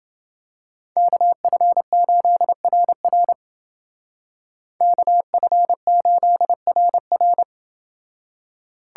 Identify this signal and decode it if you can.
Morse code, heard as a keyed tone, about 29 wpm, 700 Hz. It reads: KF8RR KF8RR